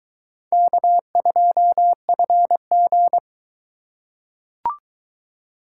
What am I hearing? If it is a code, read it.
Morse code, heard as a keyed tone, about 20 wpm, 700 Hz. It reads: K2FG